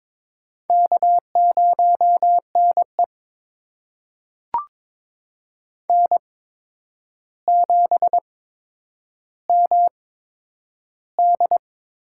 Morse code, heard as a keyed tone, about 22 wpm, spaced as for 9 wpm, 700 Hz. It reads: K0NE N7MD
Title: USA Call Signs - 4 Characters - 8x Character Spacing - 22wpm